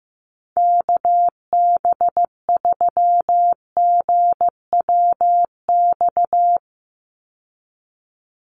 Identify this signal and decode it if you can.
Morse code, heard as a keyed tone, about 15 wpm, 700 Hz. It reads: KB3GWX
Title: USA Call Signs 15wpm